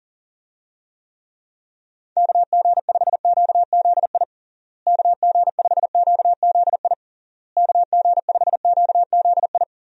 Morse code, heard as a keyed tone, about 40 wpm, 700 Hz. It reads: KG5QZI KG5QZI KG5QZI
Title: USA Call Signs - 6 Characters - Repeated 3x - 40wpm